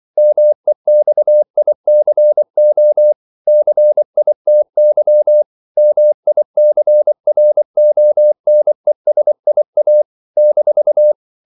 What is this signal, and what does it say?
Morse code, heard as a keyed tone, about 24 wpm, 600 Hz. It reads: MEXICO CITY MICRONESIA -
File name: CW / Practice / NationalCapitals24